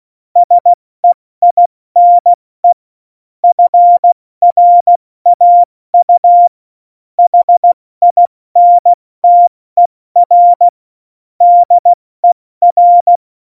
Morse code, heard as a keyed tone, about 16 wpm, 700 Hz. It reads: SEINE FRAU HINTER DER
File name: Morsekurs / Witze / Witz 21 Juli 2020, 70 F80